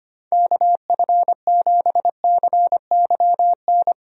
Morse code, heard as a keyed tone, about 25 wpm, 700 Hz. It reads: KF7CYN
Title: USA Call Signs - 6 Characters - 25wpm